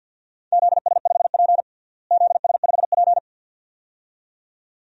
Morse code, heard as a keyed tone, about 53 wpm, 700 Hz. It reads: ZS5P ZS5P
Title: DXCC-100 Call Signs - 4 Characters - Repeated 2x - 50wpm